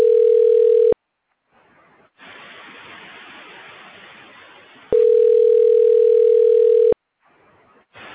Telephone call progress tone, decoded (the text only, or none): ringback tone